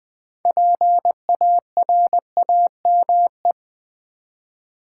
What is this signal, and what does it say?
Morse code, heard as a keyed tone, about 20 wpm, 700 Hz. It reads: PARAME
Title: Build A Word - 3 Letters At A Time - 9-Letter Words - 20wpm